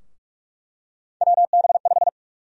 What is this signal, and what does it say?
Morse code, heard as a keyed tone, about 46 wpm, 700 Hz. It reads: WB5
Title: USA Call Sign Prefixes - 45wpm